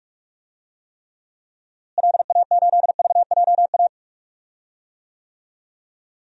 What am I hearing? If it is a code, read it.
Morse code, heard as a keyed tone, about 41 wpm, 700 Hz. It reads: LA8VJA
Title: DXCC-100 Call Signs - 6 Characters - Repeated 3x - 45wpm